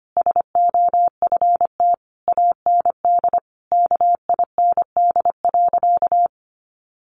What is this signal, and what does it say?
Morse code, heard as a keyed tone, about 25 wpm, 700 Hz. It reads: SOFT AND KIND.
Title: Sentences from Top 1500 Words 25wpm